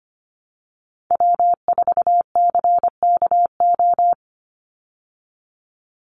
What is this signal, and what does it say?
Morse code, heard as a keyed tone, about 25 wpm, 700 Hz. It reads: W4CKO
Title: USA Call Signs 25wpm